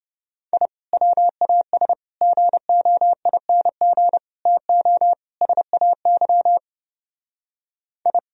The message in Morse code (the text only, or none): I WAS GOING TO SAY I